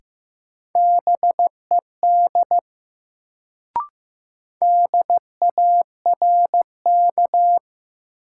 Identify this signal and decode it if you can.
Morse code, heard as a keyed tone, about 15 wpm, 700 Hz. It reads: BED DARK
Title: Sentences from Top 500 Words - New Word Drill - 15wpm